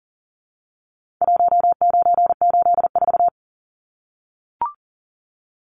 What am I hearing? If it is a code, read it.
Morse code, heard as a keyed tone, about 40 wpm, 700 Hz. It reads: 1984
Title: Numbers - Contest Style - 4-Digit 40wpm